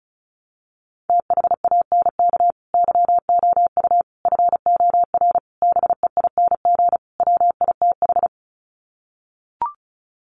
Morse code, heard as a keyed tone, about 34 wpm, 700 Hz. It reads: THANK YOU FOR BEING WITH
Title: Sets of 5 Words 35wpm